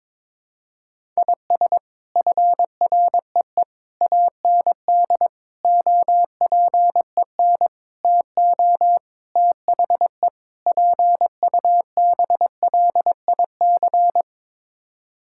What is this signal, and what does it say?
Morse code, heard as a keyed tone, about 22 wpm, 700 Hz. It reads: IS FREE AND OPEN TO THE PUBLIC
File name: practice / 22wpm / Sets of 7 Words - 22wpm